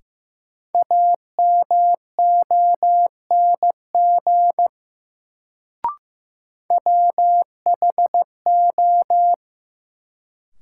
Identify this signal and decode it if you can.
Morse code, heard as a keyed tone, about 15 wpm, 700 Hz. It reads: AMONG WHO
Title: Sets of 2 Words - New Word Drill - 15wpm